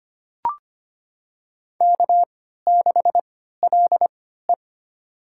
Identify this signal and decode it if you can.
Morse code, heard as a keyed tone, about 25 wpm, spaced as for 14 wpm, 700 Hz. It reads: K6LE
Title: USA Call Signs - 4 Characters - 3x Character Spacing - 25wpm